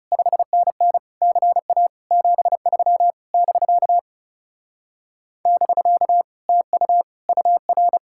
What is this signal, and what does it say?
Morse code, heard as a keyed tone, about 36 wpm, 700 Hz. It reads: 5NN CA 73 <BK> <BK> TU UR
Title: Speed-Racing Common QSO Phrases - 30wpm